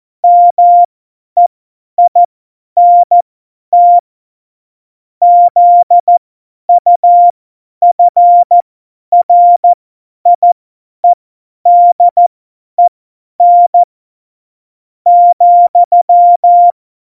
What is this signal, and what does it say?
Morse code, heard as a keyed tone, about 14 wpm, spaced as for 9 wpm, 700 Hz. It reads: MEINT ZUFRIEDEN,
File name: Morsekurs / Witze / Witz 26. Juni 2018, 50 F70